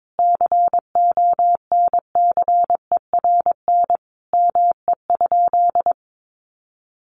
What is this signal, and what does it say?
Morse code, heard as a keyed tone, about 22 wpm, 700 Hz. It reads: CONCERN ME?
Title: Sentences from Top 1300 Words 22wpm